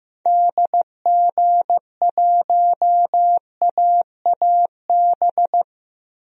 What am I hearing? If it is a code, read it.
Morse code, heard as a keyed tone, about 15 wpm, 700 Hz. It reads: DG1AAB
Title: DXCC-100 Call Signs - 6 Characters - Repeated 2x - 15wpm